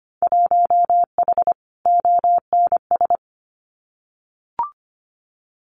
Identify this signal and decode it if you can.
Morse code, heard as a keyed tone, about 25 wpm, 700 Hz. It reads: 1H ONS
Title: Winter Field Day Exchanges 25wpm